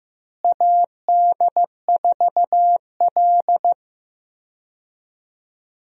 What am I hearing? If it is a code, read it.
Morse code, heard as a keyed tone, about 15 wpm, 700 Hz. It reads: AD4L
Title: USA Call Signs - 4 Characters - Repeated 3x - 15wpm